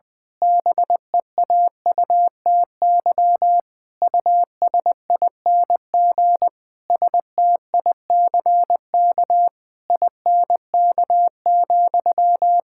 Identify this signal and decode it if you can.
Morse code, heard as a keyed tone, about 20 wpm, 700 Hz. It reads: BEAUTY USING STICK INK,
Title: Pangram Challenge v2 - 20wpm